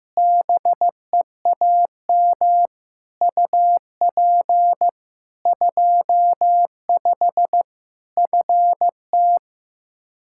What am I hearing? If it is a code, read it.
Morse code, heard as a keyed tone, about 15 wpm, 700 Hz. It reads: BEAM UP 25 FT
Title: Practice QSO - Exchange 2 - Info - 15wpm